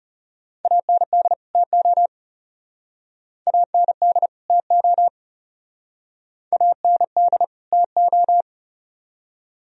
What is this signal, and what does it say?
Morse code, heard as a keyed tone, about 34 wpm, 700 Hz. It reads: AND TO AND TO AND TO
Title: Speed-Racing - Sets of 2 Words 30wpm